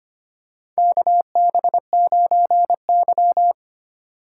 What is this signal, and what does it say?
Morse code, heard as a keyed tone, about 25 wpm, 700 Hz. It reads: KB9Y
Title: USA Call Signs - 4 Characters - 4x Character Spacing - 25wpm